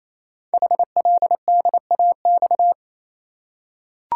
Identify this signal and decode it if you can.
Morse code, heard as a keyed tone, about 27 wpm, 700 Hz. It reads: HLDAX
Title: Code Groups - 5 Letters - 28wpm